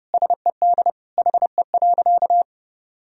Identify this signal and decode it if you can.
Morse code, heard as a keyed tone, about 29 wpm, 700 Hz. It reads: SED HE.
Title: Sentences from Top 100 Words and Abbreviations 30wpm